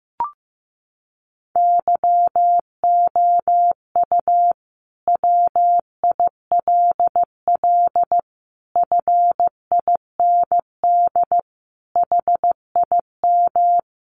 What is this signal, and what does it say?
Morse code, heard as a keyed tone, about 15 wpm, 700 Hz. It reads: YOU WILL FIND HIM
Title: Sentences from Top 1100 Words 15wpm